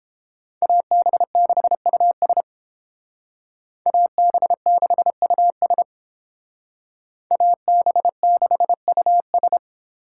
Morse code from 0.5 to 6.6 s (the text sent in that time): AB6US AB6US